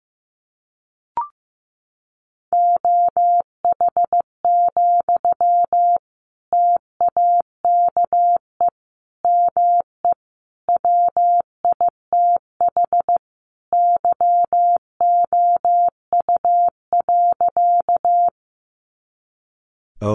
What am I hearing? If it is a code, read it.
Morse code, heard as a keyed tone, about 15 wpm, 700 Hz. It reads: OH, TAKE ME WITH YOU.
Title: Sentences from Top 500 Words 15wpm